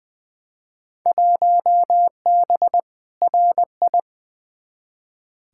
Morse code, heard as a keyed tone, about 20 wpm, 700 Hz. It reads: 1B RI
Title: Field Day Exchanges Encore - 20wpm